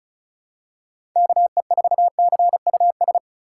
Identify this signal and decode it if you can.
Morse code, heard as a keyed tone, about 36 wpm, 700 Hz. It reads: KE4CUS
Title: USA Call Signs - 6 Characters - 3x Character Spacing - 35wpm